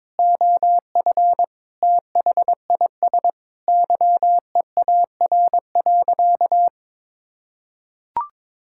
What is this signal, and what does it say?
Morse code, heard as a keyed tone, about 22 wpm, 700 Hz. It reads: OF THIS YEAR.
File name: practice / 22wpm / Sentences with 8-Word Phrases - 22wpm